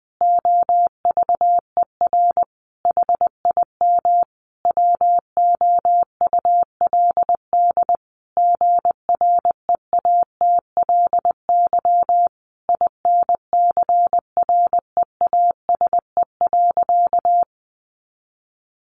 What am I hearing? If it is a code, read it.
Morse code, heard as a keyed tone, about 20 wpm, 700 Hz. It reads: OVER HIM WOULD GREATLY INCREASE.
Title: Sentences from Top 1400 Words 20wpm